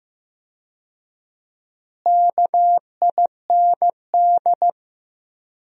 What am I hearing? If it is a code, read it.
Morse code, heard as a keyed tone, about 15 wpm, 700 Hz. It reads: KIND
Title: 4x Word-Spacing - Sentences from Top 200 Words - 15wpm